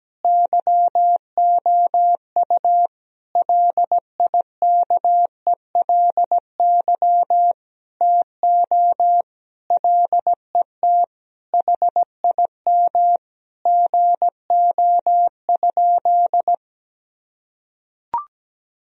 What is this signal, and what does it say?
Morse code, heard as a keyed tone, about 17 wpm, 700 Hz. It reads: YOU LIKELY TO LET HIM GO?
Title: Sentences from Top 1000 Words - Review Part 4 - 17wpm